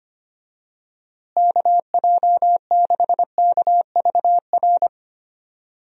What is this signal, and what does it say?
Morse code, heard as a keyed tone, about 25 wpm, 700 Hz. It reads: KJ6KVR
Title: USA Call Signs - 6 Characters - Repeated 2x - 25wpm